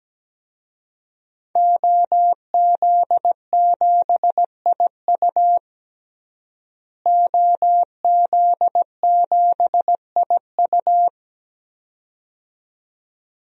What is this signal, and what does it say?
Morse code, heard as a keyed tone, about 17 wpm, 700 Hz. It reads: OZ7IU OZ7IU
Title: DXCC-100 Call Signs - 5 Characters - Repeated 2x - 17wpm